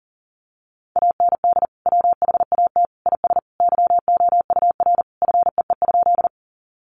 Morse code, heard as a keyed tone, about 40 wpm, 700 Hz. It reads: AND WHAT IS YOUR FEE?
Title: Sentences from Top 1500 Words 40wpm